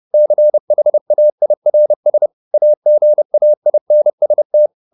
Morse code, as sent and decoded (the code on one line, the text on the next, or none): -.-. .... .- .. .-. ... / .- --. .- .. -. ... -
CHAIRS AGAINST